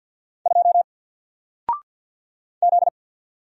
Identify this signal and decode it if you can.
Morse code, heard as a keyed tone, about 54 wpm, 700 Hz. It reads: 2 Z
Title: Single Letter-Number Alt 50wpm